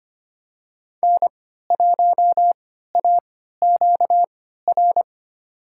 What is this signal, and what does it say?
Morse code, heard as a keyed tone, about 25 wpm, spaced as for 14 wpm, 700 Hz. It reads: N1AQR
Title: USA Call Signs - 5 Characters - 3x Character Spacing - 25wpm